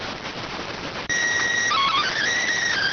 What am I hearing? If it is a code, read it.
SSTV (slow-scan television), PD 290.